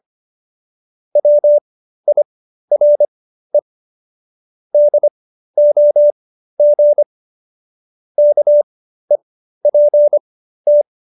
Morse code, heard as a keyed tone, about 25 wpm, spaced as for 13 wpm, 590 Hz. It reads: WIRE DOG KEPT